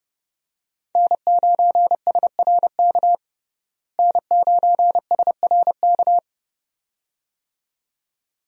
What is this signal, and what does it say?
Morse code, heard as a keyed tone, about 30 wpm, 700 Hz. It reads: N9SRK N9SRK
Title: USA Call Signs - 5 Characters - Repeated 2x - 30wpm